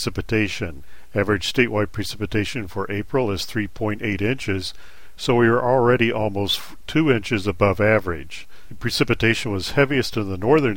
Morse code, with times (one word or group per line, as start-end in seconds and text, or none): none